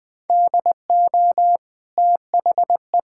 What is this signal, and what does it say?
Morse code, heard as a keyed tone, about 20 wpm, 700 Hz. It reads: DO THE
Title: Sentences from Top 1000 Words - Review Part 6 - 20wpm